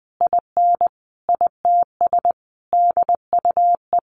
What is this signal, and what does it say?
Morse code, heard as a keyed tone, about 20 wpm, 700 Hz. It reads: IN ITS DUE
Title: Sentences from Top 900 Words 20wpm